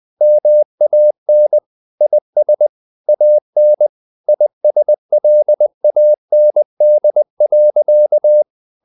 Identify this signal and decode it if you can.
Morse code, heard as a keyed tone, about 20 wpm, 600 Hz. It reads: MAN IS AN ISLAND.
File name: CW / Practice / EnglishSayings20